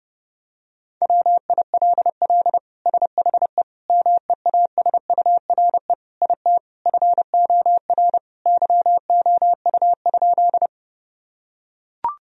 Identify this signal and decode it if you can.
Morse code, heard as a keyed tone, about 30 wpm, 700 Hz. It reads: WILL SHE MEASURE IT FOR YOU?
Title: Sentences from Top 1000 Words - Review Part 2 - 30wpm